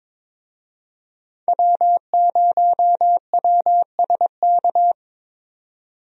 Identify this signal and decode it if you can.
Morse code, heard as a keyed tone, about 22 wpm, 700 Hz. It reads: W0WSK
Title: USA Call Signs - 5 Characters - 22wpm